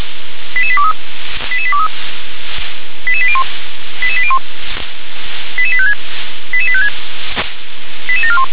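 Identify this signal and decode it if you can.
Five-tone selective call (ZVEI), heard as quick runs of five tones, tones of about 70 ms. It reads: 88823 88823 88801 88801 88856 88856 88741
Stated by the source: ZVEI five-tone call received by SDR.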